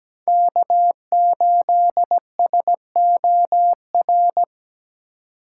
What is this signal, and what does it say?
Morse code, heard as a keyed tone, about 17 wpm, 700 Hz. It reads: K8SOR
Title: USA Call Signs - 5 Characters - 2x Character Spacing - 17wpm